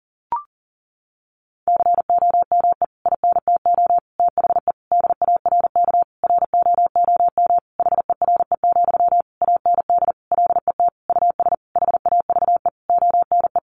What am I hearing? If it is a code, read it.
Morse code, heard as a keyed tone, about 40 wpm, 700 Hz. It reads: COME INTO THE DARK ROOM HERE, AND LET US HAVE ONE
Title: Sentences from Top 1000 Words 40wpm